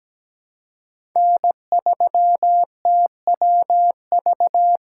Morse code, heard as a keyed tone, about 17 wpm, 700 Hz. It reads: N3TWV
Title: USA Call Signs - 2x Character Spacing - 17wpm